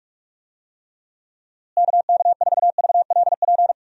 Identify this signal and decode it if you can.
Morse code, heard as a keyed tone, about 43 wpm, 700 Hz. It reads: KK4VLP
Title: USA Call Signs - 6 Characters - Repeated 3x - 45wpm